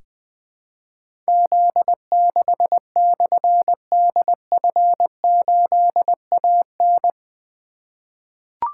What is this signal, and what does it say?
Morse code, heard as a keyed tone, about 20 wpm, 700 Hz. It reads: Z6/DF8AN
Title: Difficult Call Signs 20wpm